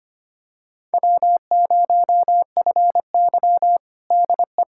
Morse code, heard as a keyed tone, about 25 wpm, 700 Hz. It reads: W0FY DE
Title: USA Callsign DE Callsign - 4 Characters - 25wpm